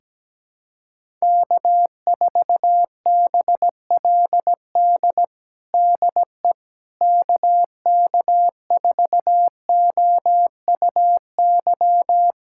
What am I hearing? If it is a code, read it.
Morse code, heard as a keyed tone, about 17 wpm, 700 Hz. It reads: K4BLD DE KK4OUY
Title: USA Callsign DE Callsign 17wpm